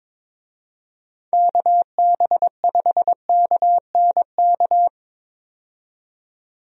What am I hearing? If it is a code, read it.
Morse code, heard as a keyed tone, about 22 wpm, 700 Hz. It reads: KB5KNK
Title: USA Call Signs - 6 Characters - 22wpm